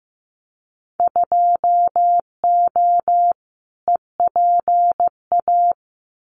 Morse code, heard as a keyed tone, about 15 wpm, 700 Hz. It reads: 2O EPA
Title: Winter Field Day Exchanges Encore 15wpm